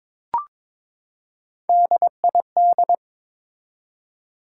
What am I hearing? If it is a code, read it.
Morse code, heard as a keyed tone, about 22 wpm, 700 Hz. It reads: DID
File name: Build A Phrase - 2-Word Phrases - 22wpm